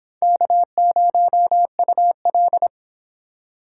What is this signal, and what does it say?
Morse code, heard as a keyed tone, about 26 wpm, 700 Hz. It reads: K0UL